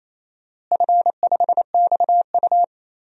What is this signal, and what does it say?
Morse code, heard as a keyed tone, about 28 wpm, 700 Hz. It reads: F5XU
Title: DXCC-100 Call Signs - 4 Characters - Repeated 2x - 28wpm